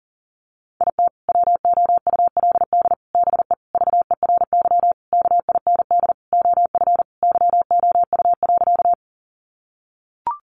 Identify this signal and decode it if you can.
Morse code, heard as a keyed tone, about 40 wpm, 700 Hz. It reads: IT WOULD BE VERY KIND OF YOU.
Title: Sentences from Top 200 Words 40wpm